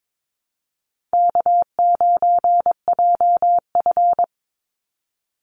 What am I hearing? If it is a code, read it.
Morse code, heard as a keyed tone, about 22 wpm, 700 Hz. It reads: K9JF
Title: Easy Call Signs - 22wpm